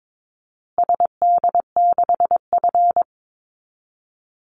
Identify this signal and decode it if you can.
Morse code, heard as a keyed tone, about 22 wpm, 690 Hz. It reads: SD6F
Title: Call Signs Repeated 2x - 22wpm